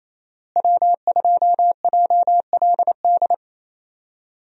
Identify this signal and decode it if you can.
Morse code, heard as a keyed tone, about 28 wpm, 700 Hz. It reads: W2JLD
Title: USA Call Signs - 5 Characters - Repeated 2x - 28wpm